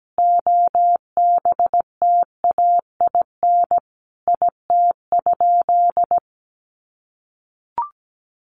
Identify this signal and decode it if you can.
Morse code, heard as a keyed tone, about 17 wpm, 700 Hz. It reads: OBTAIN IT?
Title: Sentences from Top 1300 Words 17wpm